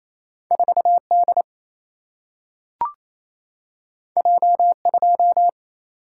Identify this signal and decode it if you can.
Morse code, heard as a keyed tone, about 28 wpm, 700 Hz. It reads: 4 D J 2
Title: Single-Letter Single-Number 28wpm